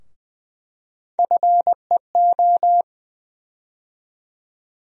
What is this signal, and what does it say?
Morse code, heard as a keyed tone, about 20 wpm, 700 Hz. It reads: FEO
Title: USA Call Sign Postfixes - 3 Letters 20wpm